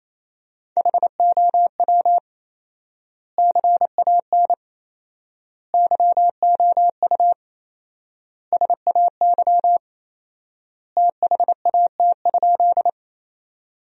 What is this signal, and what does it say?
Morse code, heard as a keyed tone, about 27 wpm, 700 Hz. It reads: HOW CAN YOU SAY THAT?
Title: 4x Word-Spacing - Sentences from Top 200 Words - 28wpm